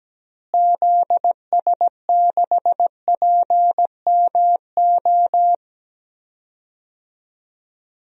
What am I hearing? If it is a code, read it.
Morse code, heard as a keyed tone, about 17 wpm, 700 Hz. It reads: ZS6PMO
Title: DXCC-100 Call Signs - 6 Characters - Repeated 2x - 17wpm